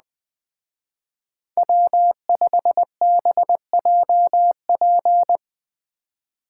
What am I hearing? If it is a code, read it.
Morse code, heard as a keyed tone, about 20 wpm, 700 Hz. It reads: W5BJP
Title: Speed Racing - USA 1x3 Call Signs 17wpm